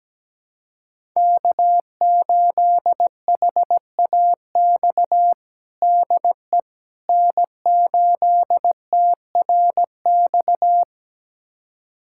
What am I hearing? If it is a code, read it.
Morse code, heard as a keyed tone, about 17 wpm, 700 Hz. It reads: K8HAX DE N8TRX
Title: USA Callsign DE Callsign - 5 Characters - 17wpm